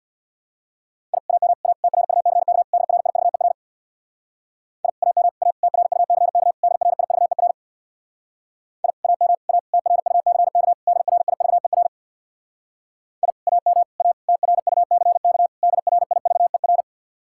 Morse code, heard as a keyed tone, about 66 wpm, 700 Hz. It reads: I AM A TRUCK DRIVER I AM A TRUCK DRIVER I AM A TRUCK DRIVER I AM A TRUCK DRIVER